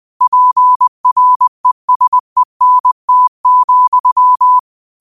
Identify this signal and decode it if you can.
Morse code, heard as a keyed tone, about 20 wpm, 1000 Hz. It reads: PRESENT,
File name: qotdfiles / 20wpm